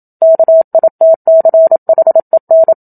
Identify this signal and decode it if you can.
Morse code, heard as a keyed tone, about 27 wpm, 650 Hz. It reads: KITCHEN
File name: Book Two / 01-Chapter 1 - 27 WPM